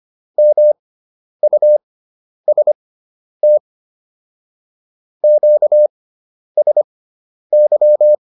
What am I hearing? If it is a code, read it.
Morse code, heard as a keyed tone, about 25 wpm, spaced as for 10 wpm, 600 Hz. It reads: MUST QSY
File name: QSO 410 10